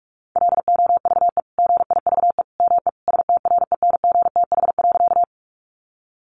Morse code, heard as a keyed tone, about 44 wpm, 700 Hz. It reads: LOVE GIVE ME STRENGTH.
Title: Sentences from Top 1300 Words 45wpm